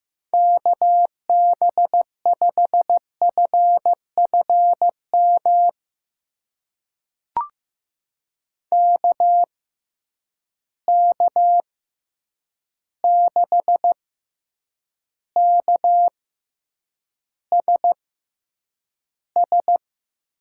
Morse code, heard as a keyed tone, about 16 wpm, spaced as for 8 wpm, 700 Hz. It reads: KB5FFM KK6KSS
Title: USA Call Signs - 6 Characters - 6x Character Spacing - 15wpm